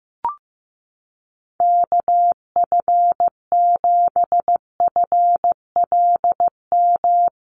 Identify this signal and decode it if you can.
Morse code, heard as a keyed tone, about 15 wpm, 700 Hz. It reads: KF7FLM
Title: USA Call Signs Repeated 3x 15wpm